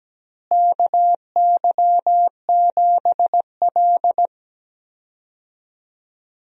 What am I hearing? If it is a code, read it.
Morse code, heard as a keyed tone, about 17 wpm, 700 Hz. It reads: KY7L
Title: USA Call Signs - 4 Characters - Repeated 2x - 17wpm